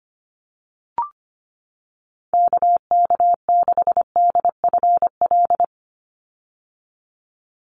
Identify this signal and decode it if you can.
Morse code, heard as a keyed tone, about 25 wpm, 700 Hz. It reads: KK6DFL
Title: USA Call Signs 25wpm